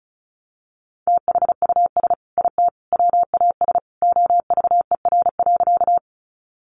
Morse code, heard as a keyed tone, about 34 wpm, 700 Hz. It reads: THUS IT WAS OVER.
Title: Sentences from Top 800 Words 35wpm